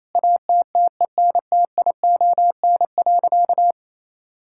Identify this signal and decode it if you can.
Morse code, heard as a keyed tone, about 28 wpm, 700 Hz. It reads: ATTENTION.
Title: Sentences from Top 1000 Words - Review Part 6 - 28wpm